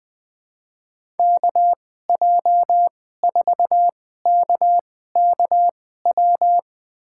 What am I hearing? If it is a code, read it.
Morse code, heard as a keyed tone, about 20 wpm, spaced as for 14 wpm, 700 Hz. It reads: KJ4KKW
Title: USA Call Signs - 2x Character Spacing - 20wpm